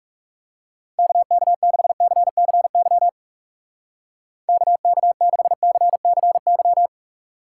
Morse code, heard as a keyed tone, about 37 wpm, 700 Hz. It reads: KK6CCY KK6CCY
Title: Speed Racing - USA 2x3 Call Signs 30wpm